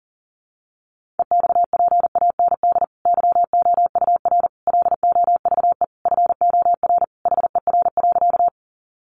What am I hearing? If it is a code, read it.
Morse code, heard as a keyed tone, about 40 wpm, 700 Hz. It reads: EXPAND YOUR LOVE FOR HER.
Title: Sentences from Top 1400 Words 40wpm